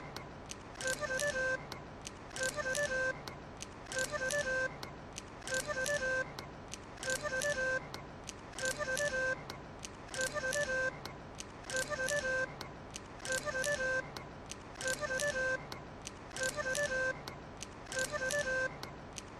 There is no code.